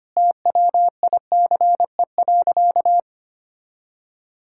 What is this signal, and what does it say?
Morse code, heard as a keyed tone, about 25 wpm, 700 Hz. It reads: TWICE.
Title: Sentences from Top 1600 Words 25wpm